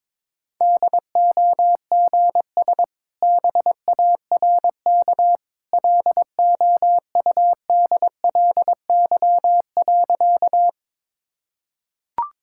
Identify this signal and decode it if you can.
Morse code, heard as a keyed tone, about 22 wpm, 700 Hz. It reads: DOGS BARK LOUDLY.
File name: practice / 22wpm / Easy Sentences with Hints - 22wpm